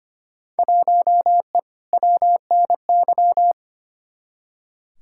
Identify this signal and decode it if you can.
Morse code, heard as a keyed tone, about 25 wpm, 700 Hz. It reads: 1E WNY
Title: Field Day Exchanges - 25wpm